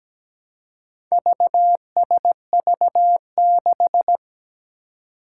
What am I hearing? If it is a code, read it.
Morse code, heard as a keyed tone, about 17 wpm, 700 Hz. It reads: VSV6
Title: Four Challenging Letter-Number Combinations 17wpm